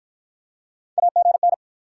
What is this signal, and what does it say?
Morse code, heard as a keyed tone, about 52 wpm, 700 Hz. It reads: AGN